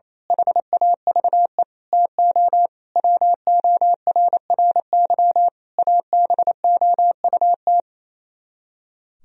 Morse code, heard as a keyed tone, about 26 wpm, 700 Hz. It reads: HAVE TO WORRY ABOUT
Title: Speed-Racing - Sets of 5 Words 28wpm